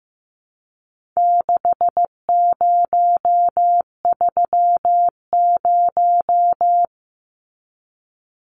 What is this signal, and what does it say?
Morse code, heard as a keyed tone, about 15 wpm, 700 Hz. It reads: 6030
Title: Numbers - 4-Digit 15wpm